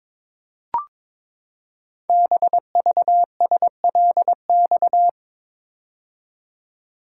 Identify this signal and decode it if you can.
Morse code, heard as a keyed tone, about 23 wpm, 700 Hz. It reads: BVSLX
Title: Code Groups - 5 Letter:Numbers - 22wpm